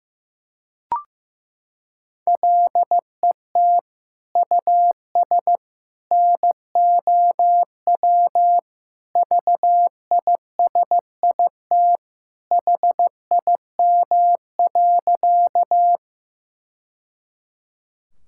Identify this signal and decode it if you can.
Morse code, heard as a keyed tone, about 15 wpm, 700 Hz. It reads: LET US NOW VISIT HIM.
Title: Sentences from Top 1000 Words - Review Part 6 - 15wpm